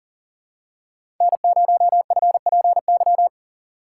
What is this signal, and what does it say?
Morse code, heard as a keyed tone, about 42 wpm, 690 Hz. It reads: N0FPY
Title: USA Call Signs - 5 Characters - 3x Character Spacing - 40wpm